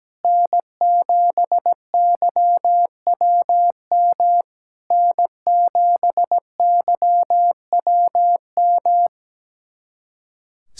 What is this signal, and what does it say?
Morse code, heard as a keyed tone, about 17 wpm, 700 Hz. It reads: N7YWM N7YWM
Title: SOTA QSO Practice 17wpm